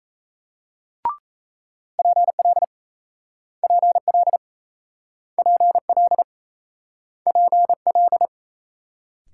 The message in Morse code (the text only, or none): PL PL PL PL